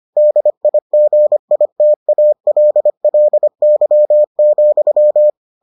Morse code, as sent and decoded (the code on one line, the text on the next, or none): -.. .. --. .. - .- .-.. .-.. -.-- --..--
DIGITALLY,